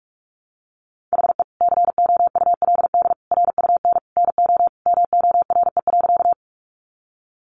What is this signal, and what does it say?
Morse code, heard as a keyed tone, about 44 wpm, 700 Hz. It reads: HE COULD RUN NO MORE.
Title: Sentences from Top 300 Words 45wpm